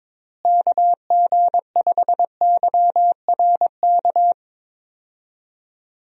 Morse code, heard as a keyed tone, about 22 wpm, 700 Hz. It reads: KG5YRK
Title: USA Call Signs - 6 Characters - Repeated 3x - 22wpm